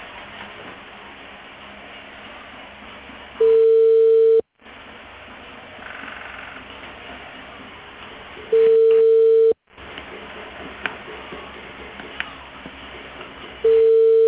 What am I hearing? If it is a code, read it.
Ringback tone (China, 450 Hz).